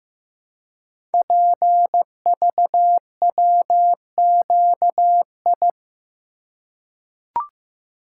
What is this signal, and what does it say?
Morse code, heard as a keyed tone, about 15 wpm, 690 Hz. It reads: PVWQI
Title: Code Groups - 5 Letters - 15wpm